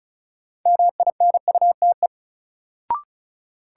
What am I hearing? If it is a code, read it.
Morse code, heard as a keyed tone, about 36 wpm, 700 Hz. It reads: MINUTE
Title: Sentences from Top 500 Words - New Word Drill - 35wpm